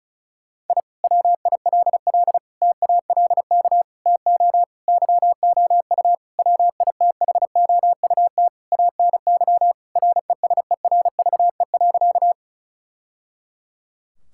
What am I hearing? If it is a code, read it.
Morse code, heard as a keyed tone, about 36 wpm, 700 Hz. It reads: I WILL TALK TO YOU WITHOUT ANY RESERVE.
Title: Sentences from Top 1000 Words - Review Part 6 - 35wpm